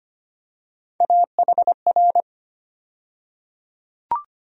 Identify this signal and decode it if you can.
Morse code, heard as a keyed tone, about 25 wpm, 700 Hz. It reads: AHR
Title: USA Call Sign Postfixes - 3 Letters 25wpm